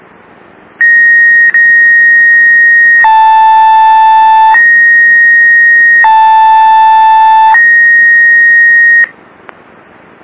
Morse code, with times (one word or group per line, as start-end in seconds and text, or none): none